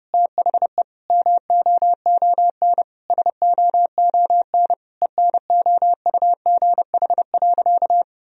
Morse code, heard as a keyed tone, about 32 wpm, 700 Hz. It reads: THE MOON SOON ENOUGH.
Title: Sentences from Top 1000 Words - Review Part 3 - 30wpm